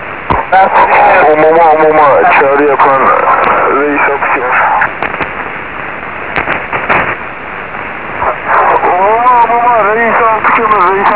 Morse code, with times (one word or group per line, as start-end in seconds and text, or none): none